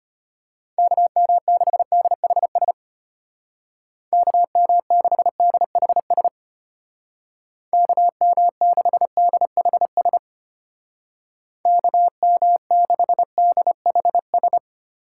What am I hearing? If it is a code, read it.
Morse code, heard as a keyed tone, about 32 wpm, 700 Hz. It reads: KM6DHS KM6DHS KM6DHS KM6DHS